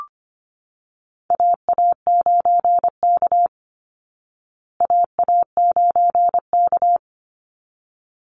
Morse code, heard as a keyed tone, about 25 wpm, 700 Hz. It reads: AA9K AA9K
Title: Call Signs Repeated 3x - 25wpm